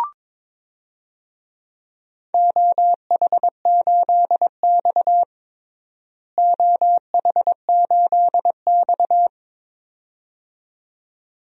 Morse code, heard as a keyed tone, about 22 wpm, 700 Hz. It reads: OH8X OH8X
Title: DXCC-100 Call Signs - 4 Characters - Repeated 2x - 22wpm